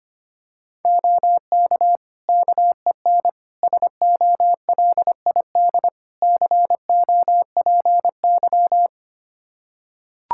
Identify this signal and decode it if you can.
Morse code, heard as a keyed tone, about 25 wpm, 700 Hz. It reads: OK KEN SOLID COPY
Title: Practice QSO - Exchange 3 - More Info - 25wpm